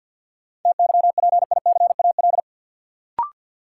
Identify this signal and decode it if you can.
Morse code, heard as a keyed tone, about 53 wpm, 700 Hz. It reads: TYPICAL
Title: Sentences from Top 1100 Words - New Word Drill - 50wpm